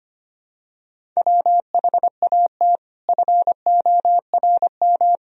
Morse code, heard as a keyed tone, about 25 wpm, 700 Hz. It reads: WHAT FORM
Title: Sentences from Top 1000 Words - Review Part 1 - 25wpm